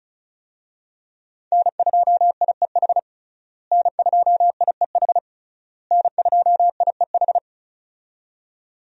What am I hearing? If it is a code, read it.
Morse code, heard as a keyed tone, about 34 wpm, 700 Hz. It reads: N2IEH N2IEH N2IEH
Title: USA Call Signs - 5 Characters - Repeated 3x - 35wpm